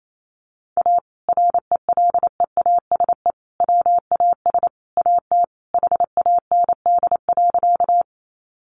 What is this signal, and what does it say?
Morse code, heard as a keyed tone, about 28 wpm, 700 Hz. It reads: A RELEASE WAS AT HAND.
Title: Sentences from Top 1100 Words 28wpm